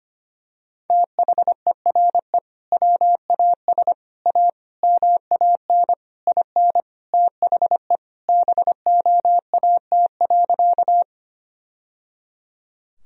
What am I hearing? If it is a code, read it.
Morse code, heard as a keyed tone, about 25 wpm, 700 Hz. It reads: THERE WAS A MAN IN THE BOAT.
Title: Sentences from Top 1000 Words - Review Part 3 - 25wpm